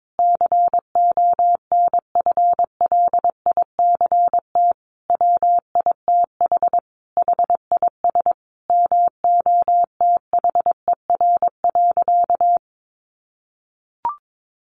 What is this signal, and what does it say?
Morse code, heard as a keyed tone, about 22 wpm, 700 Hz. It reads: CONFLICT WITH HIS MOTHER.
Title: Sentences from Top 1300 Words 22wpm